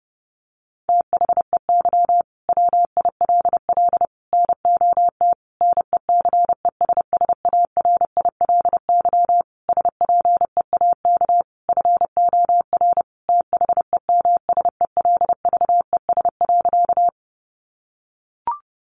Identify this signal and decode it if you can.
Morse code, heard as a keyed tone, about 30 wpm, 700 Hz. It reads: THEY WILL NOT NECESSARILY SPEAK FOR THEMSELVES.
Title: Sentences from Top 1400 Words 30wpm